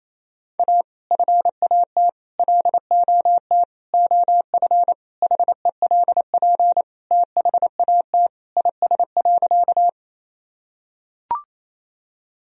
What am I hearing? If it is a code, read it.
Morse code, heard as a keyed tone, about 28 wpm, 700 Hz. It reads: A FAT LOT OF HELP THAT IS.
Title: Sentences from Top 1000 Words - Review Part 4 - 28wpm